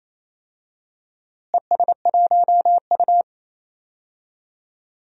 Morse code, heard as a keyed tone, about 28 wpm, 700 Hz. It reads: ES1U
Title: DXCC-100 Call Signs - 4 Characters - Repeated 3x - 28wpm